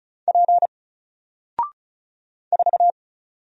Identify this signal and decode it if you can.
Morse code, heard as a keyed tone, about 36 wpm, 700 Hz. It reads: P 4
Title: Single Character (Letter:Number:Punctuation:Prosign) 35wpm